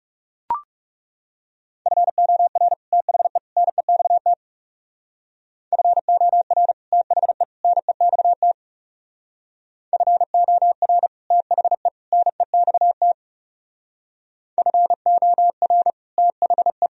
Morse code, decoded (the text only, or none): FOR THE NEXT FOR THE NEXT FOR THE NEXT FOR THE